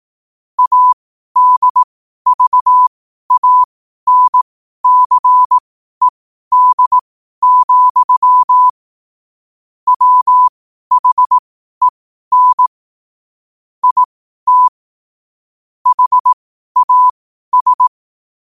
Morse code, heard as a keyed tone, about 18 wpm, spaced as for 12 wpm, 1000 Hz. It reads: ADVANCED, WHEN IT HAS
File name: qotdfiles / 12wpm